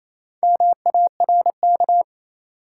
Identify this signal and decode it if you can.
Morse code, heard as a keyed tone, about 28 wpm, 700 Hz. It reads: MARK
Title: Sentences from Top 300 Words - New Word Drill - 28wpm